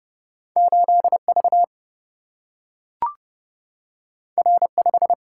Morse code, heard as a keyed tone, about 30 wpm, 700 Hz. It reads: 8V R5
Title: Single-Letter Single-Number 30wpm